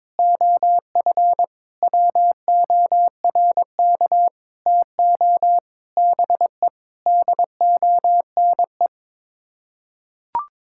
Morse code, heard as a keyed tone, about 22 wpm, 700 Hz. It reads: OF WORK TO BE DONE
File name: practice / 22wpm / Sets of 7 Words - 22wpm